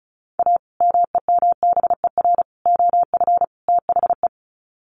Morse code, heard as a keyed tone, about 34 wpm, 700 Hz. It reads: A MEMBER OF THE
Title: Sets of 4 Words 35wpm